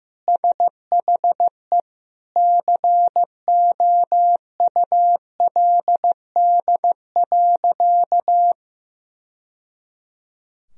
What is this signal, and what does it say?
Morse code, heard as a keyed tone, about 15 wpm, 700 Hz. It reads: SHE COULD.
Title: Sentences from Top 1000 Words - Review Part 3 - 15wpm